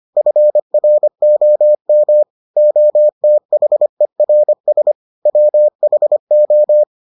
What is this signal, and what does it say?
Morse code, heard as a keyed tone, about 25 wpm, 600 Hz. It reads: FROM OTHERS WHO